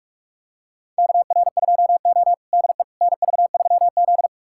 Morse code, heard as a keyed tone, about 41 wpm, 700 Hz. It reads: KR2O DE NV3Z
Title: USA Callsign DE Callsign - 4 Characters - 45wpm